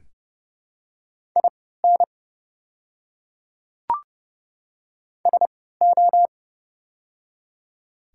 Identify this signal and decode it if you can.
Morse code, heard as a keyed tone, about 31 wpm, spaced as for 17 wpm, 700 Hz. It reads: IN SO
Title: 3x Character Spacing - Top 100 Words - 30wpm